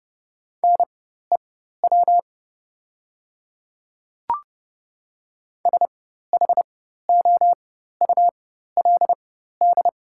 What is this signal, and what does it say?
Morse code, heard as a keyed tone, about 30 wpm, spaced as for 14 wpm, 700 Hz. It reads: NEW SHOULD
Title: 4x Character Spacing - Top 300 Words - 30wpm